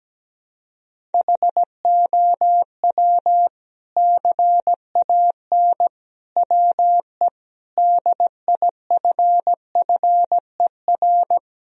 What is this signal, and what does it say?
Morse code, heard as a keyed tone, about 17 wpm, 700 Hz. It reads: HOW CAN WE DIFFER
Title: Sentences from Top 1000 Words - Review Part 1 - 17wpm